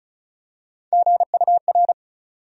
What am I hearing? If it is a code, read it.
Morse code, heard as a keyed tone, about 36 wpm, 700 Hz. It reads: GUR
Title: USA Call Sign Postfixes - 3 Letters 35wpm